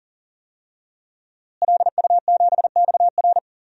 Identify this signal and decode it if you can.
Morse code, heard as a keyed tone, about 40 wpm, 700 Hz. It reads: LU7XR
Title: DXCC-100 Call Signs - 5 Characters - Repeated 3x - 40wpm